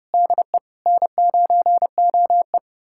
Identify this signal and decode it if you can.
Morse code, heard as a keyed tone, about 30 wpm, 700 Hz. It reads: DE N9OE
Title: USA Callsign DE Callsign - 4 Characters - 30wpm